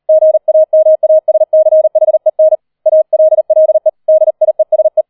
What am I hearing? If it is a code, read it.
Morse code, heard as a keyed tone, about 38 wpm, 630 Hz. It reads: GAMASCHEN ALLE DIESE
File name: 003 Dritter Streich 40wpm